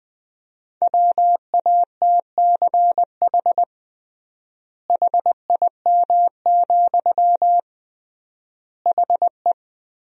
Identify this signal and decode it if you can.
Morse code, heard as a keyed tone, about 20 wpm, 700 Hz. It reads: WATCH HIM, HE
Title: 3x Word-Spacing - Sentences from Top 300 Words - 20wpm